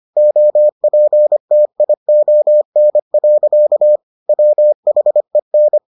Morse code, heard as a keyed tone, about 25 wpm, 600 Hz. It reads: OPTION. WHEN